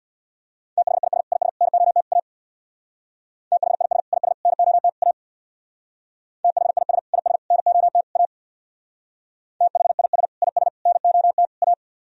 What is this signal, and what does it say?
Morse code, heard as a keyed tone, about 62 wpm, 700 Hz. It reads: THIS IS NOT A THIS IS NOT A THIS IS NOT A THIS IS NOT A
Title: Speed-Racing - Sets of 4 Words 50wpm